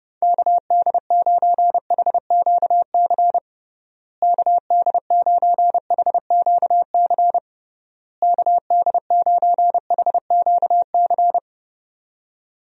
Morse code, heard as a keyed tone, about 32 wpm, 700 Hz. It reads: KD9HQC KD9HQC KD9HQC
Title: USA Call Signs - 6 Characters - Repeated 3x - 30wpm